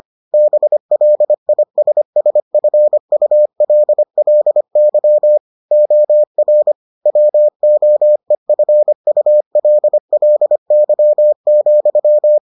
Morse code, heard as a keyed tone, about 25 wpm, 600 Hz. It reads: BLISSFULLY OR WOEFULLY,